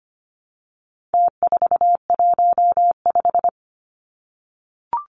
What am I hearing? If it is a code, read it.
Morse code, heard as a keyed tone, about 25 wpm, 700 Hz. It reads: T415
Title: Numbers - Contest Style - 4-Digit 25wpm